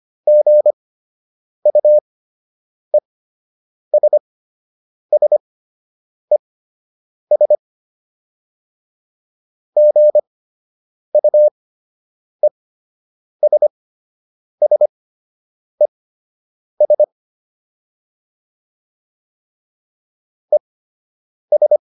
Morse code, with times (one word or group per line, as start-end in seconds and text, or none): 0.3-7.5 GUESSES
9.8-17.0 GUESSES
20.5-21.8 ES